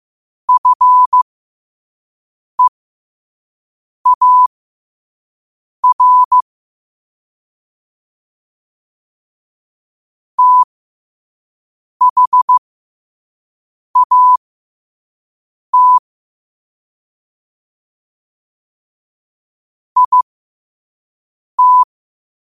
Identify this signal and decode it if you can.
Morse code, heard as a keyed tone, about 15 wpm, spaced as for 5 wpm, 1000 Hz. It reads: FEAR THAT IT
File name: qotdfiles / 5wpm